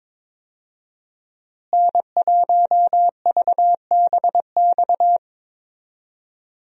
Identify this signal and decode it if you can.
Morse code, heard as a keyed tone, about 22 wpm, 700 Hz. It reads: N1VBX
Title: USA Call Signs - 5 Characters - Repeated 2x - 22wpm